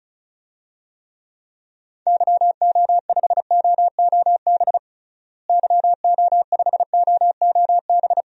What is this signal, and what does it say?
Morse code, heard as a keyed tone, about 36 wpm, 700 Hz. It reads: YO5OOB YO5OOB
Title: DXCC-100 Call Signs - 6 Characters - Repeated 3x - 35wpm